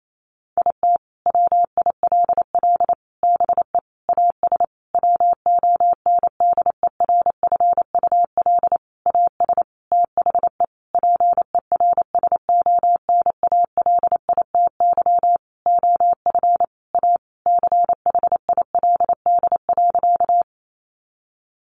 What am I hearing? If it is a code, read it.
Morse code, heard as a keyed tone, about 28 wpm, 700 Hz. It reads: IT WILL BE AS WONDERFUL AS THE PERSONALITY OF A CHILD.
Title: Sentences from Top 1400 Words 28wpm